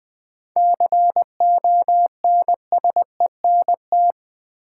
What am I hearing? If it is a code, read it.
Morse code, heard as a keyed tone, about 20 wpm, 700 Hz. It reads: CONSENT
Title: 7 Letter Words - First 3 Letters Given - 20wpm